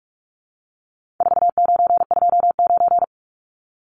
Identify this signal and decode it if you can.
Morse code, heard as a keyed tone, about 44 wpm, 700 Hz. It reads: 4929
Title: Numbers - 4-Digit 45wpm